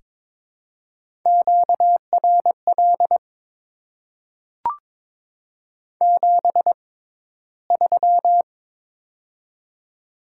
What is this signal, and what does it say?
Morse code, heard as a keyed tone, about 22 wpm, spaced as for 14 wpm, 700 Hz. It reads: QRL 73
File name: Top 20 QSO Elements - 6x Character Spacing - 22wpm